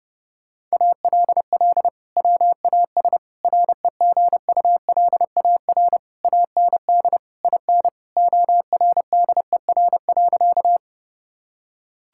Morse code, heard as a keyed tone, about 32 wpm, 700 Hz. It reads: ALL WAS REGULAR AND IN ORDER.
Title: Sentences from Top 1000 Words - Review Part 6 - 30wpm